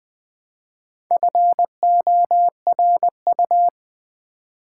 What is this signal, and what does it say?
Morse code, heard as a keyed tone, about 20 wpm, 700 Hz. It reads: FORU
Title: Build A Word - 1 Letter At A Time - 5-Letter Words - 20wpm